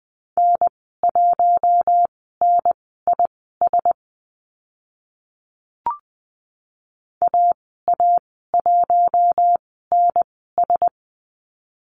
Morse code, heard as a keyed tone, about 20 wpm, spaced as for 14 wpm, 700 Hz. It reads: N1NIS AA1NS
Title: 042 Lesson 8g - Number 1 - Call Signs 20wpm